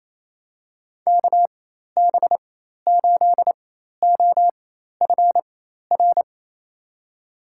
Morse code, heard as a keyed tone, about 27 wpm, spaced as for 13 wpm, 700 Hz. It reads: KB8OFR
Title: USA Call Signs - 6 Characters - 4x Character Spacing - 28wpm